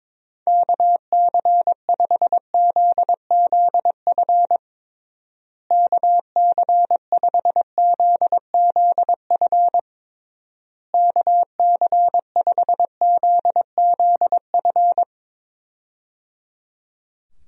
Morse code, heard as a keyed tone, about 22 wpm, 700 Hz. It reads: KC5ZZF KC5ZZF KC5ZZF
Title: USA Call Signs - 6 Characters - Repeated 3x - 22wpm